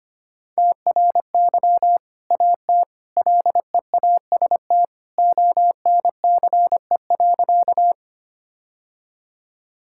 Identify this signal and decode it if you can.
Morse code, heard as a keyed tone, about 25 wpm, 700 Hz. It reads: TRY AT LEAST ONCE.
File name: practice / 25wpm / Sentences with 2-Word Phrases - 25wpm